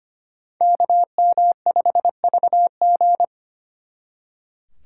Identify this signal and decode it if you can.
Morse code, heard as a keyed tone, about 31 wpm, 700 Hz. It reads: KM5VG
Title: Speed Racing - USA 2x2 Call Signs 25wpm